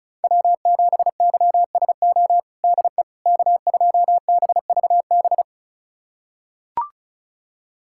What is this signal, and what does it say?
Morse code, heard as a keyed tone, about 36 wpm, 700 Hz. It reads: W7YSO DE K2BVB
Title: USA Callsign DE Callsign - 5 Characters - 35wpm